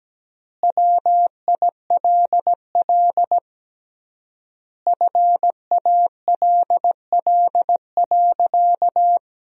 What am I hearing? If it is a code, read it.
Morse code, heard as a keyed tone, about 17 wpm, 700 Hz. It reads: WILL FALL.
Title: 3x Word-Spacing - Sentences from Top 500 Words - 17wpm